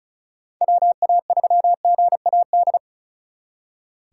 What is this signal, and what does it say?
Morse code, heard as a keyed tone, about 46 wpm, 700 Hz. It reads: WA3GAD